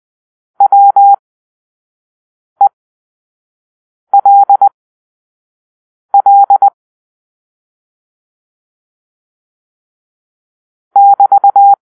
Morse code, heard as a keyed tone, about 20 wpm, spaced as for 5 wpm, 800 Hz. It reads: WELL <BT>
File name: arrl / Grp-4 Drill-1